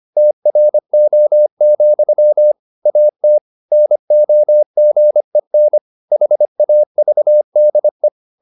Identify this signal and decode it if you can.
Morse code, heard as a keyed tone, about 25 wpm, 600 Hz. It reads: TRO, AT NOGEN HAVDE
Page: Ferie 25 wpm